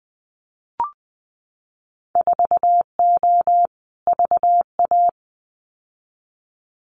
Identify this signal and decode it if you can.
Morse code, heard as a keyed tone, about 19 wpm, 700 Hz. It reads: 4O VA
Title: Winter Field Day Exchanges Encore 20wpm